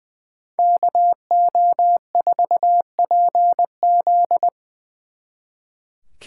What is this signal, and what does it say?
Morse code, heard as a keyed tone, about 22 wpm, 700 Hz. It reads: KO4PZ
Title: Speed Racing - USA 2x2 Call Signs 20wpm